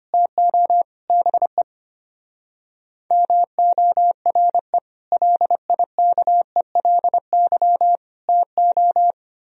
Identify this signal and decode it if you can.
Morse code, heard as a keyed tone, about 31 wpm, 700 Hz. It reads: TO BE MORE LIKELY TO
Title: Speed-Racing - Sets of 4 Words 25wpm